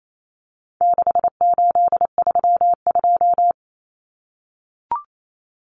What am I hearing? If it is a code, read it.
Morse code, heard as a keyed tone, about 28 wpm, 700 Hz. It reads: 6832
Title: Numbers - 4-Digit 28wpm